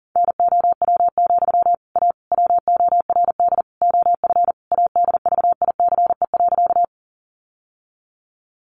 Morse code, heard as a keyed tone, about 40 wpm, 700 Hz. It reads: NOW, A WORD OF ADVICE.
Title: Sentences from Top 1100 Words 40wpm